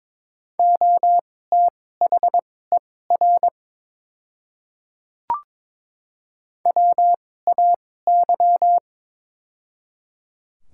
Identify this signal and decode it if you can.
Morse code, heard as a keyed tone, about 22 wpm, spaced as for 16 wpm, 700 Hz. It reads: OTHER WAY
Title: 2x Character Spacing - Top 100 Words - 22wpm